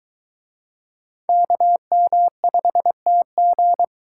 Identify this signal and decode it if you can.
Morse code, heard as a keyed tone, about 22 wpm, 700 Hz. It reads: KM5TG